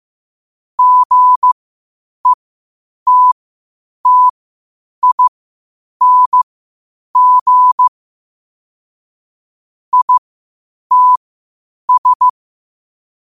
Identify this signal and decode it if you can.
Morse code, heard as a keyed tone, about 15 wpm, spaced as for 8 wpm, 1000 Hz. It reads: GETTING ITS
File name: qotdfiles / 8wpm